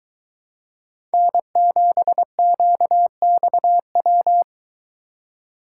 Morse code, heard as a keyed tone, about 21 wpm, 700 Hz. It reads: N7QXW